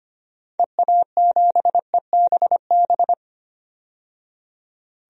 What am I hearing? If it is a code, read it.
Morse code, heard as a keyed tone, about 25 wpm, 700 Hz. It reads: EA7EBB
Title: DXCC-100 Call Signs - 6 Characters - 25wpm